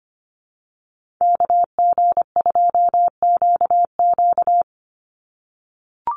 Morse code, heard as a keyed tone, about 25 wpm, 700 Hz. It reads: KG2QQ
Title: USA Call Signs 25wpm